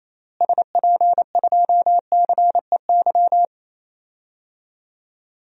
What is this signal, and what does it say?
Morse code, heard as a keyed tone, about 28 wpm, 700 Hz. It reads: SP2CEY
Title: DXCC-100 Call Signs - 6 Characters - Repeated 3x - 28wpm